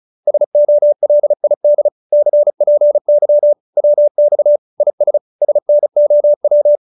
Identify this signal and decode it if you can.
Morse code, heard as a keyed tone, about 34 wpm, 590 Hz. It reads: SOLID CPY WX IS SNOW
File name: QSO302 35